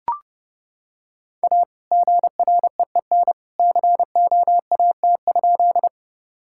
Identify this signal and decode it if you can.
Morse code, heard as a keyed tone, about 30 wpm, 700 Hz. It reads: A GREEN COAT?
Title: Sentences from Top 1600 Words 30wpm